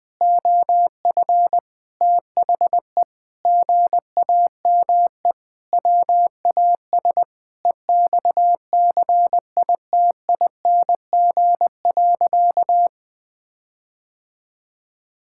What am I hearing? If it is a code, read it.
Morse code, heard as a keyed tone, about 20 wpm, 700 Hz. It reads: OF THE GAME WAS EXCITING.
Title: Sentences with 5-Word Phrases - Spoken Beginning - 20wpm